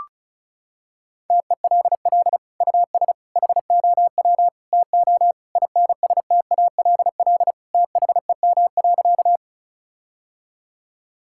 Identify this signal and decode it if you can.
Morse code, heard as a keyed tone, about 36 wpm, 700 Hz. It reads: TELL US HOW TO INSTALL THEM.
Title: Sentences from Top 1600 Words 35wpm